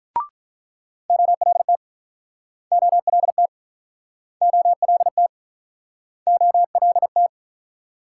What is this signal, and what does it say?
Morse code, heard as a keyed tone, about 48 wpm, 700 Hz. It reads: OLT OLT OLT OLT